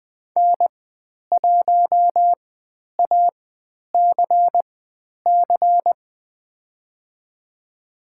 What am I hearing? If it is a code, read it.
Morse code, heard as a keyed tone, about 20 wpm, spaced as for 10 wpm, 700 Hz. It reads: N1ACC
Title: 102 Lesson 17g - Letter u - Call Signs 20wpm